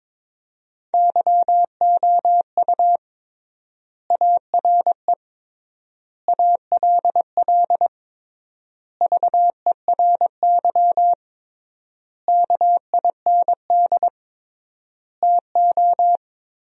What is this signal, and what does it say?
Morse code, heard as a keyed tone, about 22 wpm, 700 Hz. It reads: YOU ARE ALL VERY KIND TO
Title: 3x Word-Spacing - Sentences from Top 200 Words - 22wpm